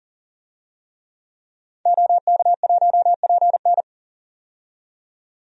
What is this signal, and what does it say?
Morse code, heard as a keyed tone, about 38 wpm, 700 Hz. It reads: OK1PN
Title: DXCC-100 Call Signs - 5 Characters - 40wpm